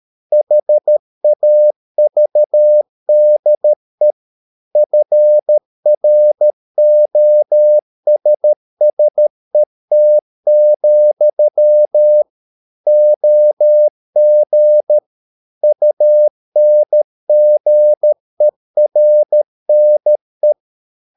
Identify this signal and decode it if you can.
Morse code, heard as a keyed tone, about 13 wpm, 600 Hz. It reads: HAVDE FROSSET, OG UNGERNE
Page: Ferie 13 wpm